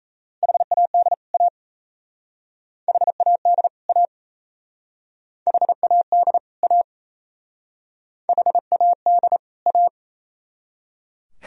Morse code, heard as a keyed tone, about 36 wpm, 690 Hz. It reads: HAD A HAD A HAD A HAD A